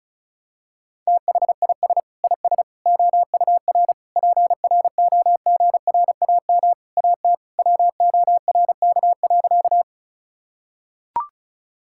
Morse code, heard as a keyed tone, about 36 wpm, 700 Hz. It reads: THIS IS OUR PROGRAM AT WORK.
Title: Sentences from Top 1000 Words - Review Part 5 - 35wpm